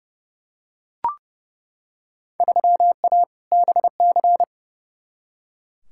Morse code, heard as a keyed tone, about 31 wpm, 700 Hz. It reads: 3A BC
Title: Field Day Exchanges - 30wpm